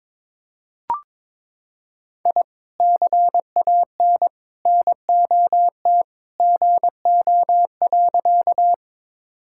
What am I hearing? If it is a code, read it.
Morse code, heard as a keyed tone, about 22 wpm, 700 Hz. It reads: I CAN NOT GO.
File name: practice / 22wpm / Sentences with 2-Word Phrases - 22wpm